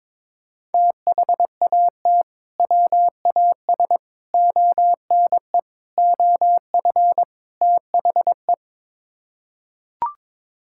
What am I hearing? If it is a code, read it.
Morse code, heard as a keyed tone, about 22 wpm, 700 Hz. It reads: THAT WAS ONE OF THE
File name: practice / 22wpm / Build A Phrase - 5-Word Phrases - 22wpm